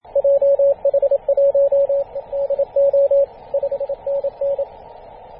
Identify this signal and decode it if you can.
Morse code, heard as a keyed tone, about 27 wpm, 560 Hz, with radio noise. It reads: JH1DO 5NN
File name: posts / 2016-feb-27-0448z-a35t-24-898mhz-cw